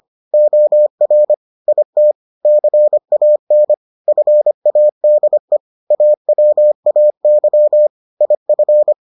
Morse code, heard as a keyed tone, about 25 wpm, 600 Hz. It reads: OR IT CAN FADE AWAY IF